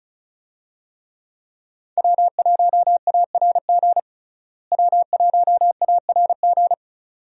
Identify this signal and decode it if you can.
Morse code, heard as a keyed tone, about 36 wpm, 700 Hz. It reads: W1ARG W1ARG
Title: USA Call Signs - 5 Characters - Repeated 2x - 35wpm